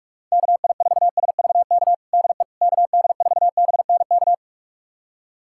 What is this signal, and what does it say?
Morse code, heard as a keyed tone, about 46 wpm, 700 Hz. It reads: KI4SVK DE KD4BNK
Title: USA Callsign DE Callsign - 6 Characters - 45wpm